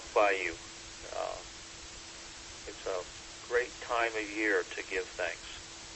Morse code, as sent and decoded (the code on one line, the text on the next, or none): none